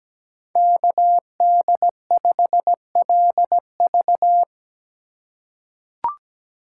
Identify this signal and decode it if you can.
Morse code, heard as a keyed tone, about 17 wpm, 700 Hz. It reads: KD5LV
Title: USA Call Signs - 5 Characters - 4x Character Spacing - 17wpm